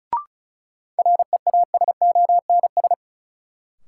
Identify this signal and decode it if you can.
Morse code, heard as a keyed tone, about 36 wpm, 700 Hz. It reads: REASONS
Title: Sets of 4 Words - New Word Drill - 35wpm